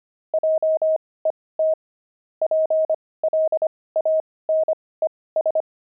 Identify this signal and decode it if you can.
Morse code, heard as a keyed tone, about 25 wpm, spaced as for 18 wpm, 630 Hz. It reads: JET PLANES
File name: pangram-fs-18-cs-25